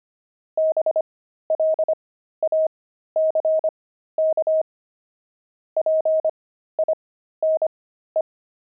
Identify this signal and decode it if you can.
Morse code, heard as a keyed tone, about 25 wpm, spaced as for 13 wpm, 630 Hz. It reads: BLACK PINE
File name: pangram-fs-13-cs-25